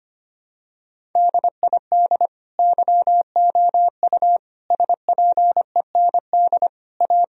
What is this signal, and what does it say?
Morse code, heard as a keyed tone, about 25 wpm, 700 Hz. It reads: DID YOU SPEND A
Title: Sentences from Top 1000 Words - Review Part 4 - 25wpm